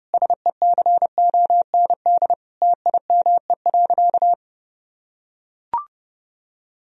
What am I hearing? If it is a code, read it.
Morse code, heard as a keyed tone, about 30 wpm, 700 Hz. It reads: SECOND TIME.
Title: Sentences from Top 1000 Words - Review Part 1 - 30wpm